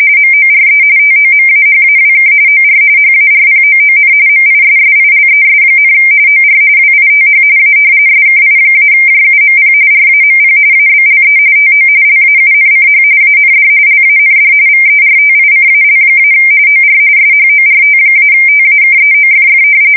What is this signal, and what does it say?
RTTY, about 45.45 baud. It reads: MORE TOMAHAWK OR HARPOON MISSILES, OR WITH A SKI-JUMP RAMP ABLE TO LAUNCH AND LAND VERTICAL- OR SHORT-TAKEOFF
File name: fsk170 / battleshipnj